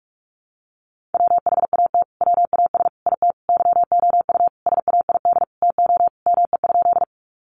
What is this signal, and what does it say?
Morse code, heard as a keyed tone, about 44 wpm, 700 Hz. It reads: WHAT WAS IT YOU SAID TO ME?
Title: Sentences from Top 200 Words 45wpm